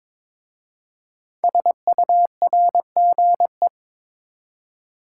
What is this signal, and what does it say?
Morse code, heard as a keyed tone, about 22 wpm, 700 Hz. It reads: SURGE
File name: 5 Letter Words - First 3 Letters Given - 22wpm